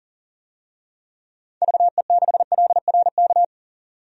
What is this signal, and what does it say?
Morse code, heard as a keyed tone, about 40 wpm, 700 Hz. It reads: VE6LRK
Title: DXCC-100 Call Signs - 6 Characters - 40wpm